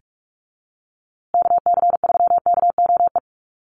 Morse code, heard as a keyed tone, about 44 wpm, 700 Hz. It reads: KC3KOE
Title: USA Call Signs Repeated 3x 45wpm